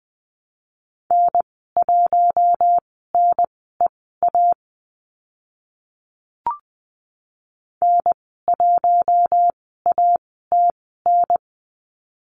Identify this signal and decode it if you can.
Morse code, heard as a keyed tone, about 20 wpm, spaced as for 14 wpm, 700 Hz. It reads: N1NEA N1ATN
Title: 042 Lesson 8g - Number 1 - Call Signs 20wpm